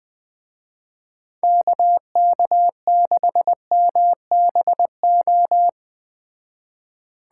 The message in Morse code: KK6MBO

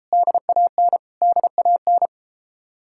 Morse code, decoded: DAN DAN

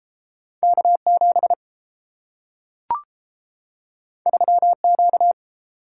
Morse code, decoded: K7 3Q

